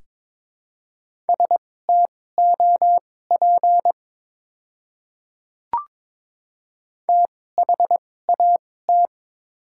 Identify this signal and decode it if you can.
Morse code, heard as a keyed tone, about 22 wpm, spaced as for 16 wpm, 700 Hz. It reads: STOP THAT
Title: 2x Character Spacing - Top 500 Words - 22wpm